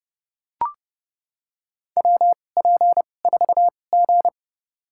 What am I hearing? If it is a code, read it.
Morse code, heard as a keyed tone, about 31 wpm, spaced as for 22 wpm, 700 Hz. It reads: WP4G
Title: USA Call Signs - 4 Characters - 2x Character Spacing - 30wpm